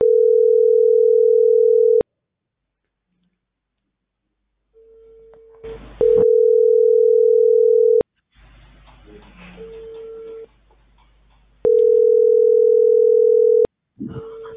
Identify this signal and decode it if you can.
Ringback tone (North America), 440 + 480 Hz.